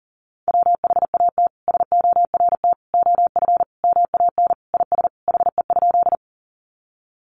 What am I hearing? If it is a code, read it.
Morse code, heard as a keyed tone, about 37 wpm, 700 Hz. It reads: WHAT SORT OF MAN IS HE?
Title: Sentences from Top 1300 Words 40wpm